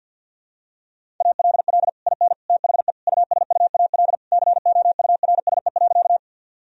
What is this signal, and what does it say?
Morse code, heard as a keyed tone, about 51 wpm, 700 Hz. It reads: ALL IN THE USUAL COURSE.
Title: Sentences from Top 1000 Words - Review Part 2 - 50wpm